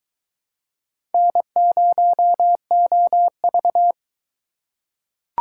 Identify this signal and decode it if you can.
Morse code, heard as a keyed tone, about 21 wpm, 700 Hz. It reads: N0OV